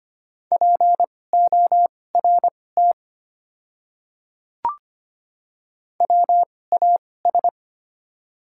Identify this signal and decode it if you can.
Morse code, heard as a keyed tone, about 25 wpm, spaced as for 18 wpm, 690 Hz. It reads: PORT WAS
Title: 2x Character Spacing - Top 200 Words - 25wpm